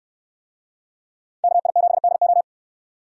Morse code, heard as a keyed tone, about 67 wpm, 700 Hz. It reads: KI7DO